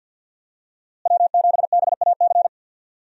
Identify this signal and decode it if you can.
Morse code, heard as a keyed tone, about 46 wpm, 700 Hz. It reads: W7BAC
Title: USA Call Signs - 5 Characters - Repeated 3x - 50wpm